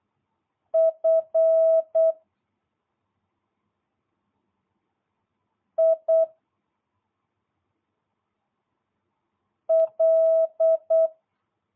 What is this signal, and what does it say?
Morse code, heard as a keyed tone, about 8 wpm, 650 Hz. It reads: F I L